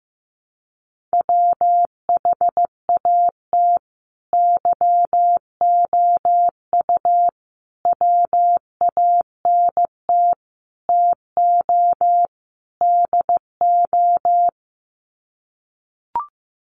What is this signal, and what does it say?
Morse code, heard as a keyed tone, about 15 wpm, 700 Hz. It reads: WHAT YOU WANT TO DO
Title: Sets of 5 Words Encore 15wpm